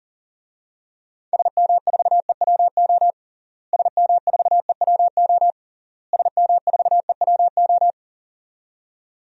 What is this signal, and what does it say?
Morse code, heard as a keyed tone, about 40 wpm, 700 Hz. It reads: SM4EWO SM4EWO SM4EWO
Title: DXCC-100 Call Signs - 6 Characters - Repeated 3x - 40wpm